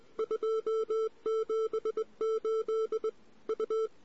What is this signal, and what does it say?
Morse code, heard as a keyed tone, about 22 wpm, 500 Hz. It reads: 278 U
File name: CW / 2008-ss-cw-k5tr-op-n5rz -14:00